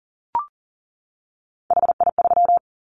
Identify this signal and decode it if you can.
Morse code, heard as a keyed tone, about 38 wpm, 690 Hz. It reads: HI3
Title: Callsign Prefixes for Top 100 DXCC - 40wpm